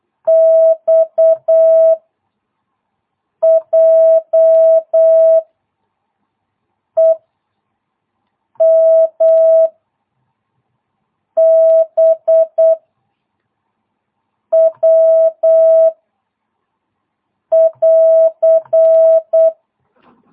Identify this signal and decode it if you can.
Morse code, heard as a keyed tone, about 8 wpm, 650 Hz. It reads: X J E M B W <AR>